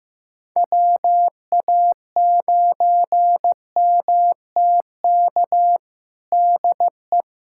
Morse code, heard as a keyed tone, about 15 wpm, 700 Hz. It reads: WA9MTK DE
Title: USA Callsign DE Callsign - 6 Characters - 15wpm